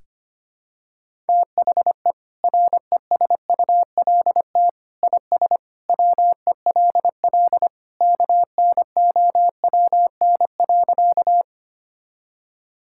Morse code, heard as a keyed tone, about 25 wpm, 700 Hz. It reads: THE RESULT IS WELL KNOWN.
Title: Sentences from Top 1000 Words - Review Part 5 - 25wpm